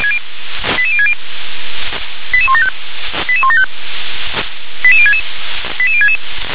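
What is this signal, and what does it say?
Five-tone selective call (ZVEI), heard as quick runs of five tones, tones of about 80 ms. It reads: 88060 88175 88175 88066 88066
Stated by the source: ZVEI five-tone call 88018 received on an SDR.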